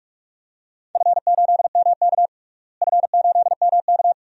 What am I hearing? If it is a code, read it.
Morse code, heard as a keyed tone, about 45 wpm, 700 Hz. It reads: F8MK F8MK
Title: DXCC-100 Call Signs - 4 Characters - Repeated 2x - 45wpm